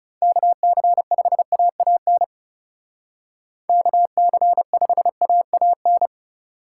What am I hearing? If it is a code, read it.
Morse code, heard as a keyed tone, about 37 wpm, 700 Hz. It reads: KC5AAN KC5AAN
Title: Speed Racing - USA 2x3 Call Signs 30wpm